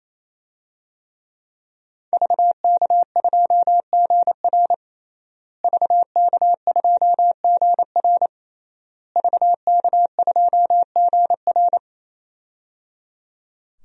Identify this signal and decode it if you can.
Morse code, heard as a keyed tone, about 28 wpm, 700 Hz. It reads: VK2GR VK2GR VK2GR
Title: DXCC-100 Call Signs - 5 Characters - Repeated 3x - 28wpm